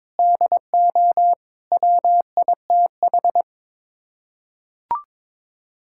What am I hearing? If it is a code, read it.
Morse code, heard as a keyed tone, about 22 wpm, 700 Hz. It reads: DO WITH
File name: practice / 22wpm / Sets of 8 Words - 22wpm